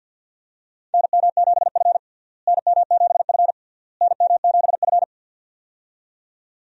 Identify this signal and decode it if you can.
Morse code, heard as a keyed tone, about 52 wpm, 700 Hz. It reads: NM7F NM7F NM7F
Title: USA Call Signs - 4 Characters - Repeated 3x - 50wpm